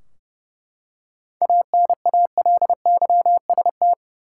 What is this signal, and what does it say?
Morse code, heard as a keyed tone, about 30 wpm, 700 Hz. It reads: ANALYST
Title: Practice QSO - Exchange 3 - More Info - 30wpm